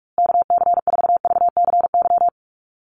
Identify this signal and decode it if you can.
Morse code, heard as a keyed tone, about 44 wpm, 700 Hz. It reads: KC4VCY
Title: USA Call Signs Repeated 2x 45wpm